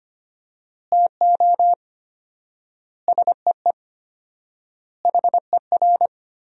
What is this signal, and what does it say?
Morse code, heard as a keyed tone, about 25 wpm, 700 Hz. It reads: TO SEE HER
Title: 4x Word-Spacing - Sentences from Top 200 Words - 25wpm